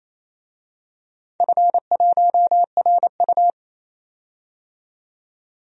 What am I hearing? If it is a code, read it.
Morse code, heard as a keyed tone, about 27 wpm, 700 Hz. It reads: F1RU
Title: DXCC-100 Call Signs - 4 Characters - 28wpm